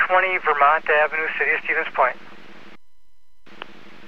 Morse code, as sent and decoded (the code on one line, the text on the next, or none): none